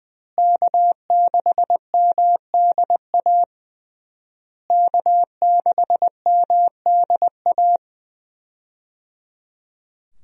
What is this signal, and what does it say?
Morse code, heard as a keyed tone, about 20 wpm, 700 Hz. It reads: K6MDA K6MDA
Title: USA Call Signs - 5 Characters - Repeated 3x - 20wpm